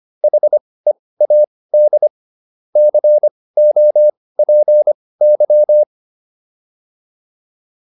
Morse code, heard as a keyed tone, about 25 wpm, spaced as for 18 wpm, 600 Hz. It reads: HEAD COPY